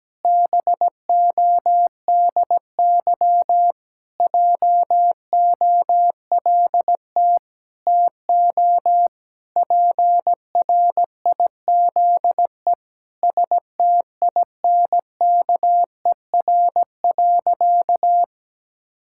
Morse code, heard as a keyed tone, about 17 wpm, 700 Hz. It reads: BODY JOLT TO PRIZE STINKER.
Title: Pangram Challenge v2 - 17wpm